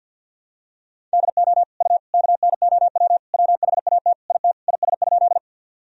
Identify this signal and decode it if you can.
Morse code, heard as a keyed tone, about 48 wpm, 700 Hz. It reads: DO U KNOW WHAT IT IS?
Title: Sentences from Top 100 Words and Abbreviations 50wpm